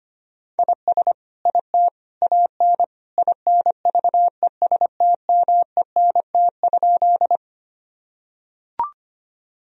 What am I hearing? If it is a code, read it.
Morse code, heard as a keyed tone, about 25 wpm, 700 Hz. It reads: IS IT AN INVESTMENT?
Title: Sentences from Top 1000 Words - Review Part 4 - 25wpm